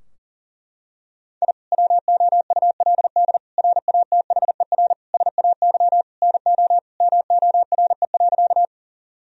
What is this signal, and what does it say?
Morse code, heard as a keyed tone, about 40 wpm, 700 Hz. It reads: I WOULD RATHER SAY NO MORE.
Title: Sentences from Top 1000 Words - Review Part 4 - 40wpm